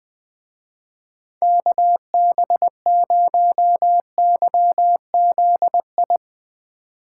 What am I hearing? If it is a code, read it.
Morse code, heard as a keyed tone, about 20 wpm, 700 Hz. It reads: KB0YZI
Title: Speed Racing - USA 2x3 Call Signs 20wpm